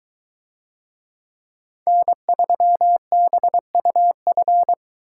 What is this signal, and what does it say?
Morse code, heard as a keyed tone, about 22 wpm, 700 Hz. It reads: N3BUF